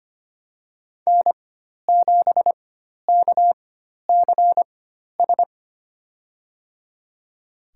Morse code, heard as a keyed tone, about 25 wpm, spaced as for 12 wpm, 700 Hz. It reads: N7KCS
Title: USA Call Signs - 4x Character Spacing - 25wpm